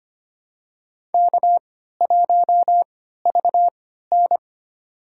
Morse code, heard as a keyed tone, about 25 wpm, spaced as for 14 wpm, 700 Hz. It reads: K1VN
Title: USA Call Signs - 4 Characters - 3x Character Spacing - 25wpm